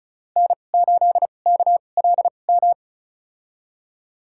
Morse code, heard as a keyed tone, about 36 wpm, spaced as for 25 wpm, 700 Hz. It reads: N8KLM
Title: USA Call Signs - 5 Characters - 2x Character Spacing - 35wpm